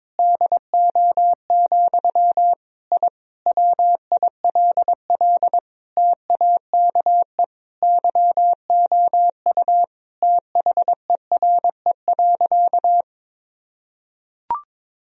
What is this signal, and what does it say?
Morse code, heard as a keyed tone, about 22 wpm, 700 Hz. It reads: DO, I WILL TAKE YOU THERE.
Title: Sentences from Top 1000 Words - Review Part 3 - 22wpm